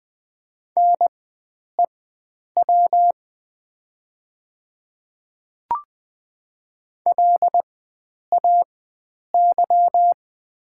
Morse code, heard as a keyed tone, about 20 wpm, spaced as for 9 wpm, 700 Hz. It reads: NEW LAY
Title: 4x Character Spacing - Top 500 Words - 20wpm